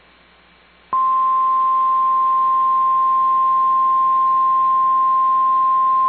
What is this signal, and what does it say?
Emergency Alert System (SAME) tones (the 1050 Hz attention tone). It is heard from 0.9 to 4.9 s.
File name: EAS Logs / KMNR